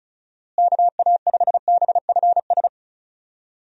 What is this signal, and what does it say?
Morse code, heard as a keyed tone, about 36 wpm, 700 Hz. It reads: KA5BFS
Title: USA Call Signs - 6 Characters - Repeated 2x - 35wpm